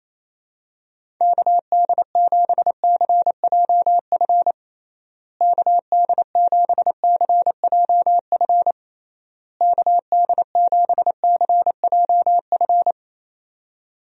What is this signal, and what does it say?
Morse code, heard as a keyed tone, about 28 wpm, 700 Hz. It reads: KD7CJF KD7CJF KD7CJF
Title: USA Call Signs - 6 Characters - Repeated 3x - 28wpm